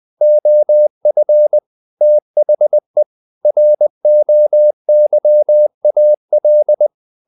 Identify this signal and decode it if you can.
Morse code, heard as a keyed tone, about 20 wpm, 600 Hz. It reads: OF THE ROYAL
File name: the old man and the sea / the old man and the sea part 1 20wpm